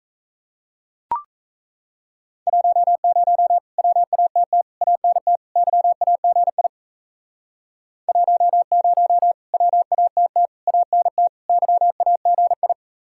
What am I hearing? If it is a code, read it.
Morse code, heard as a keyed tone, about 37 wpm, 700 Hz. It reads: 10 WATT ANT YAGI 10 WATT ANT YAGI